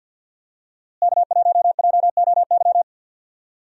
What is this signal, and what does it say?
Morse code, heard as a keyed tone, about 48 wpm, 700 Hz. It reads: K1JOY
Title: USA Call Signs - 5 Characters - Repeated 2x - 50wpm